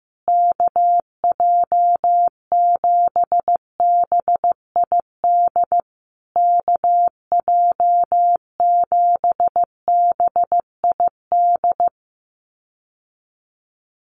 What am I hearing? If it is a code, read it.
Morse code, heard as a keyed tone, about 15 wpm, 700 Hz. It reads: KJ7BID KJ7BID
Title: USA Call Signs Repeated 3x 15wpm